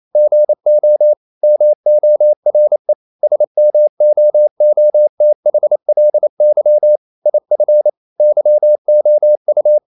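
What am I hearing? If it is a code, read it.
Morse code, heard as a keyed tone, about 28 wpm, 600 Hz. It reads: GO MORE SMOOTHLY IF YOU
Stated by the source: Morse code practice at 28 wpm, text of Ken Follett's novel Never in English, from CW med Gnister.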